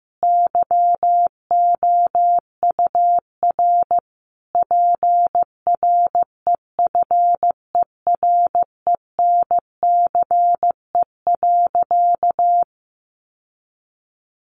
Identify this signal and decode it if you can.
Morse code, heard as a keyed tone, about 15 wpm, 700 Hz. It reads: YOUR PREFERENCE.
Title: Sentences from Top 1500 Words 15wpm